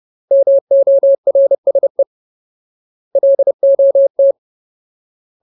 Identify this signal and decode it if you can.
Morse code, heard as a keyed tone, about 30 wpm, 550 Hz. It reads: MORSE LOT
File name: Ultimate Word List / 30WPM Ultimate